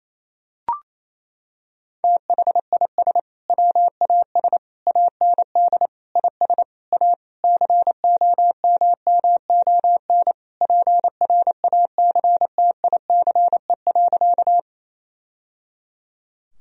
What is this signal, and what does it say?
Morse code, heard as a keyed tone, about 28 wpm, 700 Hz. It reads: THIS WAS AND IS A COMMON PRACTICE.
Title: Sentences from Top 1000 Words - Review Part 4 - 28wpm